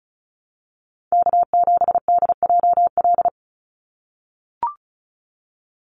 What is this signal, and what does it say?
Morse code, heard as a keyed tone, about 34 wpm, 700 Hz. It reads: K7DJL
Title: NA SOTA Chaser Callsigns 35wpm